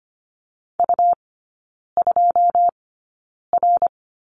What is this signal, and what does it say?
Morse code, heard as a keyed tone, about 25 wpm, spaced as for 9 wpm, 700 Hz. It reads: U2R
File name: Lesson 17e - Letter u - Three Characters 25wpm